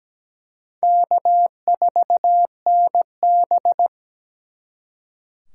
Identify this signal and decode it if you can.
Morse code, heard as a keyed tone, about 20 wpm, 700 Hz. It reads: K4NB
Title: Speed Racing - USA 1x2 Call Signs 17wpm